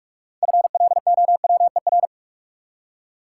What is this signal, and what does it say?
Morse code, heard as a keyed tone, about 44 wpm, 700 Hz. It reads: FLOWER
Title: Six Letter Words - 45wpm